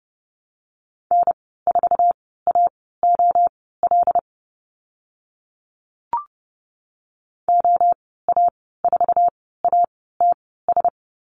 Morse code, heard as a keyed tone, about 30 wpm, spaced as for 17 wpm, 700 Hz. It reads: N4AOL OA4ATS
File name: Lesson 14g - Number 2 - Call Signs 30wpm